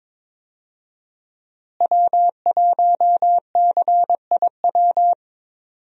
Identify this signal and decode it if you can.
Morse code, heard as a keyed tone, about 22 wpm, 700 Hz. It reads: W1CIW
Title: USA Call Signs - 5 Characters - Repeated 2x - 22wpm